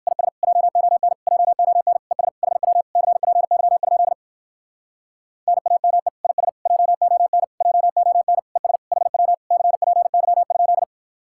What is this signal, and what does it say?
Morse code, heard as a keyed tone, about 58 wpm, 700 Hz. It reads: IS JON JON ES HW CPY? NAME IS JON JON ES HW CPY?